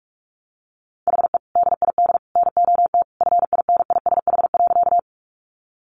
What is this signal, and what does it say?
Morse code, heard as a keyed tone, about 44 wpm, 700 Hz. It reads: HE DID NOT FINISH.
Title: Sentences from Top 1200 Words 45wpm